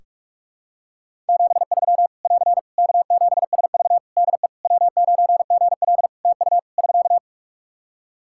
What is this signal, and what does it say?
Morse code, heard as a keyed tone, about 46 wpm, 700 Hz. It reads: 73 <AR> K7SV DE W9GL TU <SK>
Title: Practice QSO - Exchange 4 - Ending - 45wpm